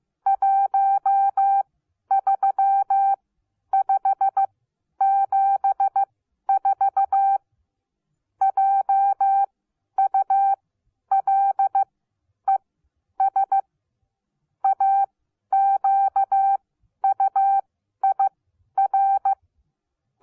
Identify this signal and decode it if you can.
Morse code, heard as a keyed tone, about 15 wpm, spaced as for 10 wpm, 780 Hz. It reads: 13574 JULES AQUIR
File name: Onlinelehrgang / morsen / DJ4UF Morsen14